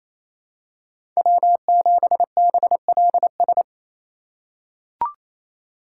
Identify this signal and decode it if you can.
Morse code, heard as a keyed tone, about 28 wpm, 700 Hz. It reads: W7BLS
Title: USA Call Signs - 5 Characters - 4x Character Spacing - 28wpm